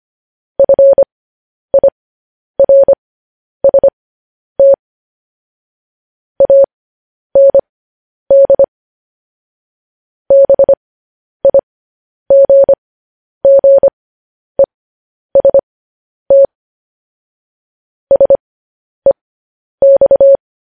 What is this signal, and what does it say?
Morse code, heard as a keyed tone, about 25 wpm, spaced as for 10 wpm, 550 Hz. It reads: FIRST AND BIGGEST SEX